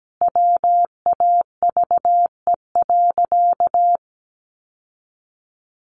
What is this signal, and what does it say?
Morse code, heard as a keyed tone, about 17 wpm, 700 Hz. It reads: WAVE.
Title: Sentences from Top 1400 Words 17wpm